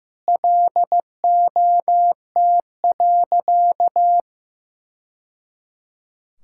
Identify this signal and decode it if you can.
Morse code, heard as a keyed tone, about 15 wpm, 700 Hz. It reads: LOT.
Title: Sentences from Top 1000 Words - Review Part 5 - 15wpm